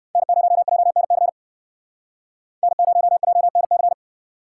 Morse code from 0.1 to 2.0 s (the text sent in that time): N0JNC